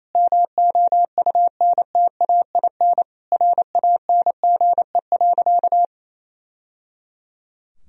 Morse code, heard as a keyed tone, about 28 wpm, 700 Hz. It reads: MOUNTAIN RANGE.
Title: Sentences from Top 1000 Words - Review Part 4 - 28wpm